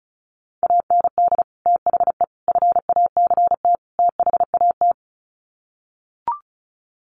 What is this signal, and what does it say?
Morse code, heard as a keyed tone, about 34 wpm, 700 Hz. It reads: AND THE FACT THAT
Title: Sets of 4 Words 35wpm